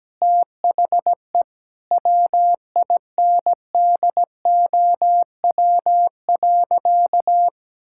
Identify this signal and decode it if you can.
Morse code, heard as a keyed tone, about 17 wpm, 700 Hz. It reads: THE WINDOW.
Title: Sentences from Top 1600 Words 17wpm